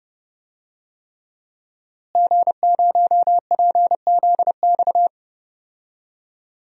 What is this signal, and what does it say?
Morse code, heard as a keyed tone, about 30 wpm, 700 Hz. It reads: G0PZX
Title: DXCC-100 Call Signs - 5 Characters - Repeated 3x - 30wpm